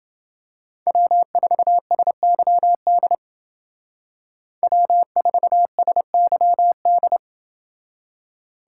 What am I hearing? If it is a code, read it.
Morse code, heard as a keyed tone, about 28 wpm, 700 Hz. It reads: W4SYD W4SYD